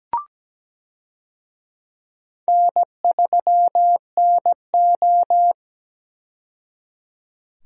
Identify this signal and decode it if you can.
Morse code, heard as a keyed tone, about 17 wpm, 700 Hz. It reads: N3NO
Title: USA Call Signs - 4 Characters - 17wpm